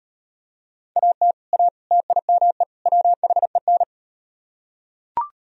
Morse code, heard as a keyed tone, about 38 wpm, 700 Hz. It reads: AT A TIME WHEN